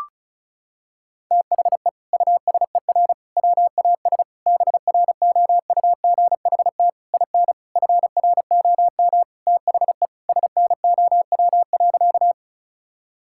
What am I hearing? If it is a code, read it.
Morse code, heard as a keyed tone, about 36 wpm, 700 Hz. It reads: THE USER WAS BROUGHT IN FROM THE SNOW.
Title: Sentences from Top 1000 Words - Review Part 5 - 35wpm